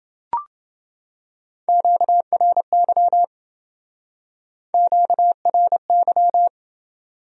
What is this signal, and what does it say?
Morse code, heard as a keyed tone, about 28 wpm, 700 Hz. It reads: QRY QRY